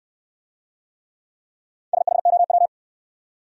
Morse code, heard as a keyed tone, about 65 wpm, 700 Hz. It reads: SHOW